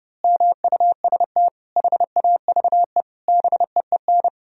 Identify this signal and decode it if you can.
Morse code, heard as a keyed tone, about 32 wpm, 700 Hz. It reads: MUST HAVE BEEN
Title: Sentences from Top 1000 Words - Review Part 4 - 30wpm